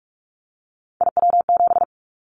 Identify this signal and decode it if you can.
Morse code, heard as a keyed tone, about 44 wpm, 700 Hz. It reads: IW7
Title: Callsign Prefixes for Top 25 DXCC - 45wpm